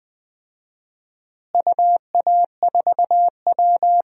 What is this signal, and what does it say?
Morse code, heard as a keyed tone, about 21 wpm, 700 Hz. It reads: UA4W
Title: DXCC-100 Call Signs - 4 Characters - Repeated 2x - 20wpm